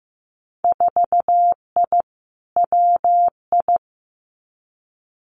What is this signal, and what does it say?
Morse code, heard as a keyed tone, about 15 wpm, 700 Hz. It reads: 4I WI
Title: Winter Field Day Exchanges 15wpm